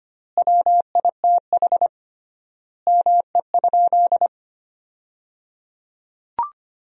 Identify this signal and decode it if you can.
Morse code, heard as a keyed tone, about 25 wpm, 700 Hz. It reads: WITH ME?
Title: 3x Word-Spacing - Sentences from Top 400 Words - 25wpm